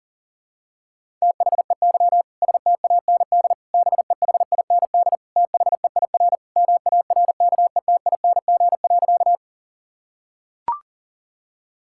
Morse code, heard as a keyed tone, about 37 wpm, 700 Hz. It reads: THEY STAND BEHIND THEIR MARKETING.
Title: Sentences from Top 1000 Words - Review Part 5 - 40wpm